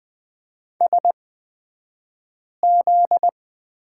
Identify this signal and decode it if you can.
Morse code, heard as a keyed tone, about 20 wpm, spaced as for 5 wpm, 700 Hz. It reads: SZ
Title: 242 Lesson 39d - Letter z - Two Characters 20wpm